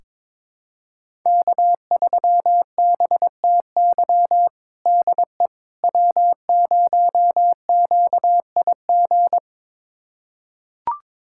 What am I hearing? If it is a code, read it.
Morse code, heard as a keyed tone, about 22 wpm, 700 Hz. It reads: K3BTY DE W0QIG
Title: USA Callsign DE Callsign - 5 Characters - 22wpm